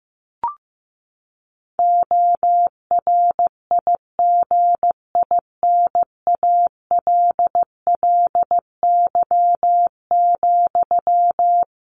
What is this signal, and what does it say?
Morse code, heard as a keyed tone, about 15 wpm, 700 Hz. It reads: ORIGINALLY,
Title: Sentences from Top 1500 Words 15wpm